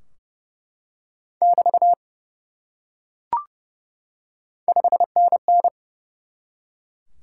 Morse code, heard as a keyed tone, about 28 wpm, 700 Hz. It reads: <BT> 5NN
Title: Top 90 QSO Elements 30wpm